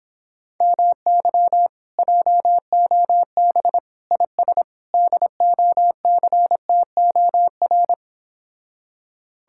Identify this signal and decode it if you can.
Morse code, heard as a keyed tone, about 26 wpm, 700 Hz. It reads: MY JOB IS DOCTOR
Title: Speed-Racing Common QSO Phrases Encore - 22wpm